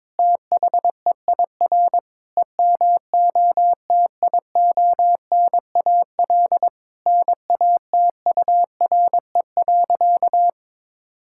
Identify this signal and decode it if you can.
Morse code, heard as a keyed tone, about 21 wpm, 700 Hz. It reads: THEIR EMOTIONAL NATURE.
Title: Sentences from Top 1000 Words - Review Part 5 - 22wpm